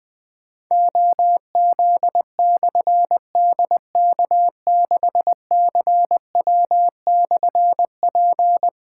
Morse code, heard as a keyed tone, about 20 wpm, 700 Hz. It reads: OZ/DK6CW/P
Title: Difficult Call Signs 20wpm